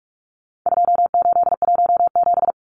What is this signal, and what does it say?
Morse code, heard as a keyed tone, about 44 wpm, 700 Hz. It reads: 2817
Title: Numbers - 4-Digit 45wpm